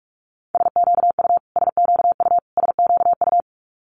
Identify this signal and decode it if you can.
Morse code, heard as a keyed tone, about 46 wpm, 700 Hz. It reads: SQU SQU SQU
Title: Multi-letter Phonemes Repeated 3x 45wpm